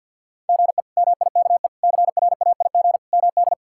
Morse code, heard as a keyed tone, about 46 wpm, 700 Hz. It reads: GE MIKE CRAIG MD
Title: K1USN SST QSO Practice - 50wpm